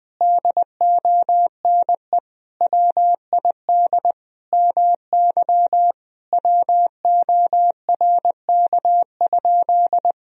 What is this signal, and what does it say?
Morse code, heard as a keyed tone, about 20 wpm, 700 Hz. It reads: DONE WID MY WORK?
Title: Sentences from Top 500 Words and Abbreviations 20wpm